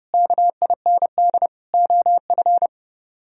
Morse code, heard as a keyed tone, about 35 wpm, 700 Hz. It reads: KIND OF
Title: Speed-Racing - Sets of 3 Words 30wpm